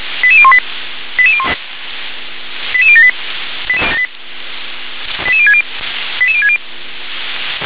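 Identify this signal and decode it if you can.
Five-tone selective call (ZVEI), heard as quick runs of five tones, tones of about 70 ms. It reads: 88018 88078 88060 88060
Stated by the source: ZVEI five-tone selective call received on SDR.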